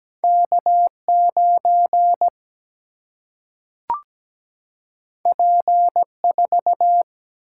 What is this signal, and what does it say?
Morse code, heard as a keyed tone, about 17 wpm, 700 Hz. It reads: K9 P4
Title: Single-Letter Single-Number 17wpm